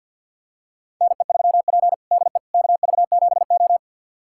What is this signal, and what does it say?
Morse code, heard as a keyed tone, about 46 wpm, 700 Hz. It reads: NE3P DE KV7O
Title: USA Callsign DE Callsign - 4 Characters - 50wpm